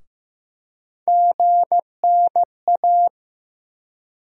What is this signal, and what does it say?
Morse code, heard as a keyed tone, about 15 wpm, 700 Hz. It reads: GNA
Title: USA Call Sign Postfixes - 3 Letters 15wpm